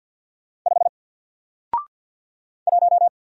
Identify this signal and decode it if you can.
Morse code, heard as a keyed tone, about 48 wpm, 700 Hz. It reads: 5 1
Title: Single Letter-Number Alt 50wpm